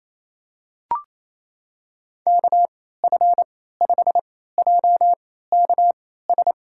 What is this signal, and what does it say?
Morse code, heard as a keyed tone, about 28 wpm, spaced as for 16 wpm, 700 Hz. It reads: KF5JKS
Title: USA Call Signs - 6 Characters - 3x Character Spacing - 28wpm